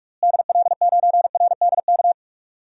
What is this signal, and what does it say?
Morse code, heard as a keyed tone, about 42 wpm, 700 Hz. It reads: DL9RDK
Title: DXCC-100 Call Signs - 6 Characters - Repeated 3x - 45wpm